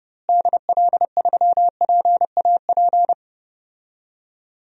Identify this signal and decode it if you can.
Morse code, heard as a keyed tone, about 32 wpm, 700 Hz. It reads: DL3PAP
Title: DXCC-100 Call Signs - 6 Characters - Repeated 2x - 30wpm